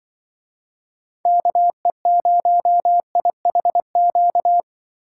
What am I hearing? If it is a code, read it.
Morse code, heard as a keyed tone, about 24 wpm, 700 Hz. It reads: KE0IHQ
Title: Speed Racing - USA 2x3 Call Signs 20wpm